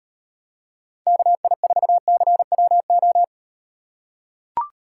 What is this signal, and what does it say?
Morse code, heard as a keyed tone, about 35 wpm, 700 Hz. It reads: KI4CWO